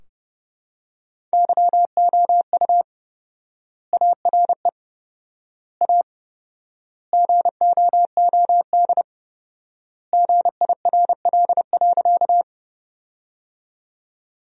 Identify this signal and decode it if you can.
Morse code, heard as a keyed tone, about 30 wpm, 700 Hz. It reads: YOU ARE A GOOD GIRL.
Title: 4x Word-Spacing - Sentences from Top 400 Words - 30wpm